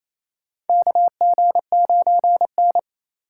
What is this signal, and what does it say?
Morse code, heard as a keyed tone, about 27 wpm, 700 Hz. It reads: KG9N
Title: USA Call Signs - 4 Characters - 2x Character Spacing - 28wpm